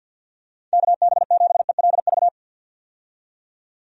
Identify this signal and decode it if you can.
Morse code, heard as a keyed tone, about 48 wpm, 700 Hz. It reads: KB7ELV
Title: USA Call Signs - 6 Characters - 50wpm